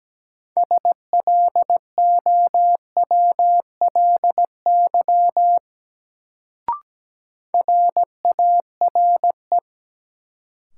Morse code, heard as a keyed tone, about 17 wpm, 700 Hz. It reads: SLOWLY RARE
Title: Sentences from Top 1100 Words - New Word Drill Alt - 17wpm